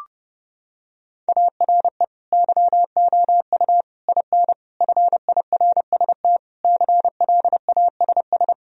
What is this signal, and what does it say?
Morse code, heard as a keyed tone, about 30 wpm, 700 Hz. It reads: ARE YOU IN FIRST CLASS
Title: Sentences from Top 1000 Words - Review Part 2 - 30wpm